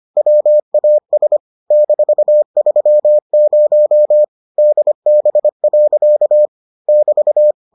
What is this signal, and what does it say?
Morse code, heard as a keyed tone, about 25 wpm, 600 Hz. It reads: WAS -30 DB. <BT>